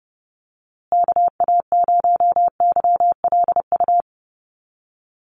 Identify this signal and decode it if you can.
Morse code, heard as a keyed tone, about 30 wpm, 700 Hz. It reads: KA0YLU
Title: USA Call Signs Repeated 2x 30wpm